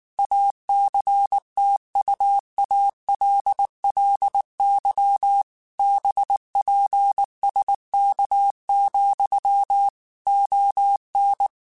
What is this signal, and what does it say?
Morse code, heard as a keyed tone, about 19 wpm, 800 Hz. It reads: ACTUALLY BPSK, ON